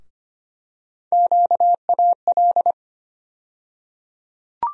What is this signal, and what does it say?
Morse code, heard as a keyed tone, about 25 wpm, 700 Hz. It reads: QAL
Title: USA Call Sign Postfixes - 3 Letters 25wpm